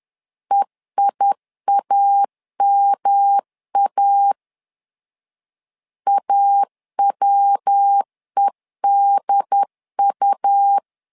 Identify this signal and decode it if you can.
Morse code, heard as a keyed tone, about 10 wpm, 800 Hz. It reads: EIAMA AWEDU
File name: LEZIONI / 3 MWRO / 3 MW 10wpm 1959